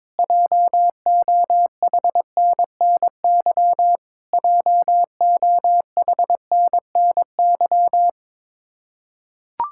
Morse code, heard as a keyed tone, about 22 wpm, 700 Hz. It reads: JOHNNY JOHNNY
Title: Practice QSO - Exchange 1 - Beginning - 22wpm